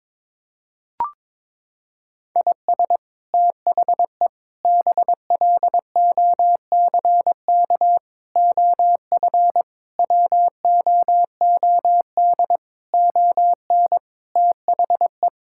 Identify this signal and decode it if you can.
Morse code, heard as a keyed tone, about 22 wpm, 700 Hz. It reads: IS THE BLOCK OF WOOD ON THE
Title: Sentences from Top 1600 Words 22wpm